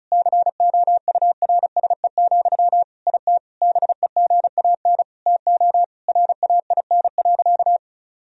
Morse code, heard as a keyed tone, about 34 wpm, 700 Hz. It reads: COURSE, IT BEGAN TO RAIN.
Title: Sentences from Top 1000 Words - Review Part 1 - 35wpm